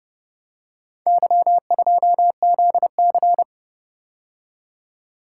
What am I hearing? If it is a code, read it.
Morse code, heard as a keyed tone, about 30 wpm, 700 Hz. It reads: Y2ZC